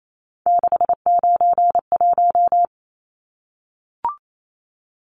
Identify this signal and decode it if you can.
Morse code, heard as a keyed tone, about 28 wpm, 700 Hz. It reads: 691
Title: Numbers - 3-Digit 28wpm